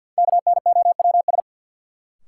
Morse code, heard as a keyed tone, about 47 wpm, 700 Hz. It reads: KNOWS
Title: Sentences from Top 1000 Words - Review Part 1 - New Word Drill - 50wpm